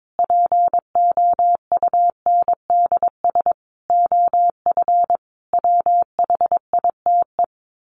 Morse code, heard as a keyed tone, about 22 wpm, 700 Hz. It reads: POUNDS OF WHITE
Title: A Sound Of Thunder 22wpm